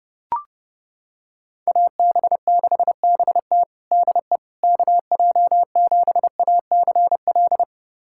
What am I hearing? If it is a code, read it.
Morse code, heard as a keyed tone, about 32 wpm, 700 Hz. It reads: AB6BT DE KJ7ACL
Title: USA Callsign DE Callsign 30wpm